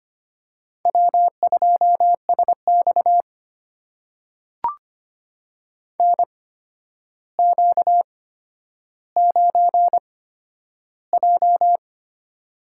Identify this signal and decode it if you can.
Morse code, heard as a keyed tone, about 25 wpm, spaced as for 11 wpm, 700 Hz. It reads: W2SX NQ9J
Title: USA Call Signs - 8x Character Spacing - 25wpm